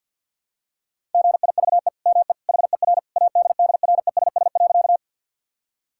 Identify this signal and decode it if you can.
Morse code, heard as a keyed tone, about 46 wpm, 700 Hz. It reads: GIVE ME HER ADDRESS.
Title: Sentences from Top 1000 Words - Review Part 6 - 50wpm